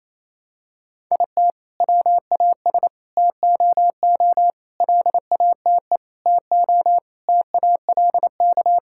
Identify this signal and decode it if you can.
Morse code, heard as a keyed tone, about 28 wpm, 700 Hz. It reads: IT WAS TOO LATE TO TALK
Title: Sentences from Top 1600 Words 28wpm